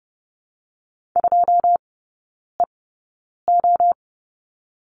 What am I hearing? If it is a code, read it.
Morse code, heard as a keyed tone, about 30 wpm, spaced as for 9 wpm, 700 Hz. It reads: 2EO
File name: 081 Lesson 14e - Number 2 - Three Characters 30wpm